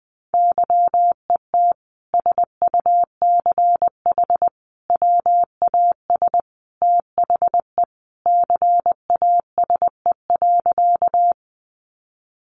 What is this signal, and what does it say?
Morse code, heard as a keyed tone, about 20 wpm, 700 Hz. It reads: YET SUCH WAS THE CASE.
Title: Sentences from Top 700 Words 20wpm